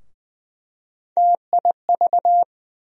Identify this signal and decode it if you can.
Morse code, heard as a keyed tone, about 20 wpm, 700 Hz. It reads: TIV
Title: USA Call Sign Postfixes - 3 Letters 20wpm